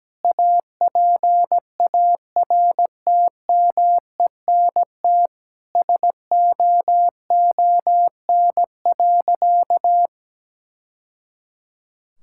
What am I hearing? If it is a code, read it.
Morse code, heard as a keyed tone, about 17 wpm, 700 Hz. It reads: APARTMENT SOON.
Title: Sentences from Top 1000 Words - Review Part 6 - 17wpm